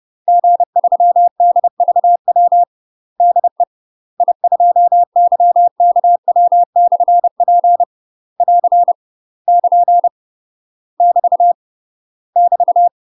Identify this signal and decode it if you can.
Morse code, heard as a keyed tone, about 30 wpm, 700 Hz. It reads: G3DVW DE I2YKW/P <AR> <KN> <BT> <BT>
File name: UK MASTER QSOs TRAINING/30WPM